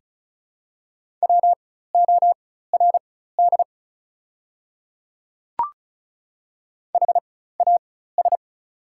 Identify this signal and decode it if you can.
Morse code, heard as a keyed tone, about 36 wpm, spaced as for 16 wpm, 690 Hz. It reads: WORD HAS
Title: 4x Character Spacing - Top 100 Words - 35wpm